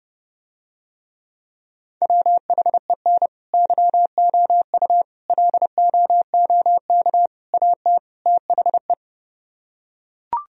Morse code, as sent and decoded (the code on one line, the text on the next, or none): .-- .... -. / -.-- --- ..- / .-.. --- --- -.- / .- - / - .... .
WHEN YOU LOOK AT THE